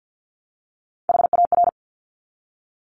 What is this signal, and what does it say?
Morse code, heard as a keyed tone, about 50 wpm, 700 Hz. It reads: HAR
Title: Top Three Letter Combinations in Words 50wpm